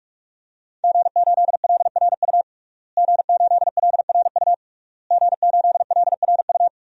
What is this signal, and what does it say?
Morse code, heard as a keyed tone, about 46 wpm, 700 Hz. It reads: G8LRU G8LRU G8LRU
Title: DXCC-100 Call Signs - 5 Characters - Repeated 3x - 45wpm